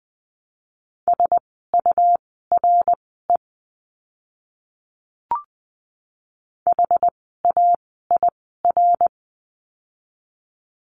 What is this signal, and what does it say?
Morse code, heard as a keyed tone, about 20 wpm, spaced as for 14 wpm, 700 Hz. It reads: SURE HAIR
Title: 101 Lesson 17f - Letter u - Words 20wpm